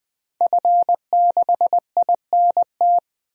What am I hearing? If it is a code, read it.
Morse code, heard as a keyed tone, about 20 wpm, 700 Hz. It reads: F6INT
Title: DXCC-100 Call Signs - 5 Characters - 20wpm